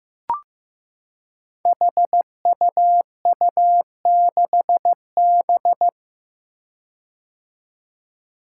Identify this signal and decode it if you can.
Morse code, heard as a keyed tone, about 15 wpm, 700 Hz. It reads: HUU6B
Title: Code Groups - 5 Letter:Numbers - 15wpm